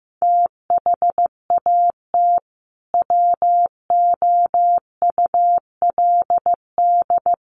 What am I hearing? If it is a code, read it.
Morse code, heard as a keyed tone, about 15 wpm, 700 Hz. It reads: THAT WOULD
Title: Sets of 2 Words 15wpm